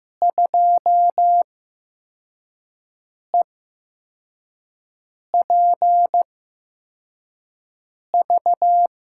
Morse code, heard as a keyed tone, about 15 wpm, spaced as for 4 wpm, 700 Hz. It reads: 2EPV
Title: USA Call Signs - 8x Character Spacing - 15wpm